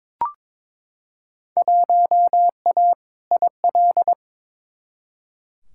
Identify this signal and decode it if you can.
Morse code, heard as a keyed tone, about 22 wpm, 700 Hz. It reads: 1A IL
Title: Field Day Exchanges Encore - 22wpm